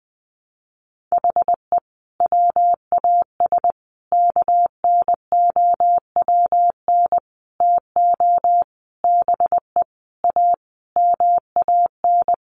Morse code, heard as a keyed tone, about 20 wpm, 700 Hz. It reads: HE WAS KNOWN TO BE A MAN
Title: Sentences from Top 1200 Words 20wpm